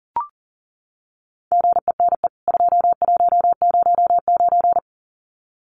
Morse code, heard as a keyed tone, about 38 wpm, 700 Hz. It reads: GENE 2109
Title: CWT Exchange 40wpm